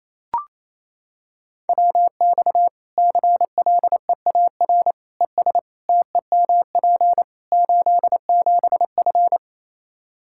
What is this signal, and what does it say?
Morse code, heard as a keyed tone, about 27 wpm, 700 Hz. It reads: WX CLEAR ES TEMP 87F
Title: Practice QSO - Exchange 2 - Info - 28wpm